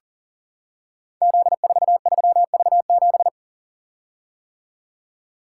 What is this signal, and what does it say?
Morse code, heard as a keyed tone, about 42 wpm, 690 Hz. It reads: Z43V7
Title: Code Groups - 5 Letter:Numbers - 40wpm